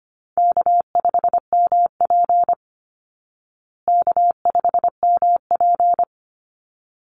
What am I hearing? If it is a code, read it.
Morse code, heard as a keyed tone, about 25 wpm, 700 Hz. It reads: K5MP K5MP
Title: Call Signs Repeated 3x - 25wpm